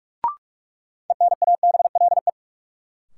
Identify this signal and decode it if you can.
Morse code, heard as a keyed tone, about 42 wpm, 700 Hz. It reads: ENABLE
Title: Sentences from Top 1500 Words - New Word Drill Alt - 45wpm